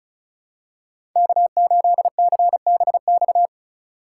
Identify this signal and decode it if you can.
Morse code, heard as a keyed tone, about 34 wpm, 700 Hz. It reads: K8CBX
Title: USA Call Signs - 4x Character Spacing - 35wpm